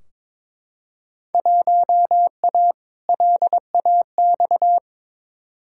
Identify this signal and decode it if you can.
Morse code, heard as a keyed tone, about 23 wpm, 700 Hz. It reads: 1A LAX
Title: Field Day Exchanges - 22wpm